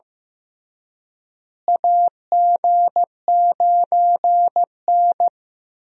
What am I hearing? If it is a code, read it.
Morse code, heard as a keyed tone, about 16 wpm, 700 Hz. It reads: AG9N
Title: USA Call Signs - 4 Characters - Repeated 3x - 15wpm